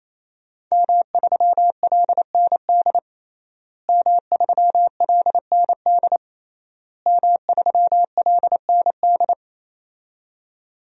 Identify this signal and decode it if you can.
Morse code, heard as a keyed tone, about 28 wpm, 690 Hz. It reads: M3LND M3LND M3LND
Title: DXCC-100 Call Signs - 5 Characters - Repeated 3x - 28wpm